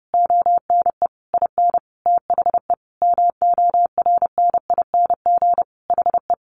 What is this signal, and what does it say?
Morse code, heard as a keyed tone, about 30 wpm, 700 Hz. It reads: ONE IN THE MORNING HE
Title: Sentences from Top 1300 Words 30wpm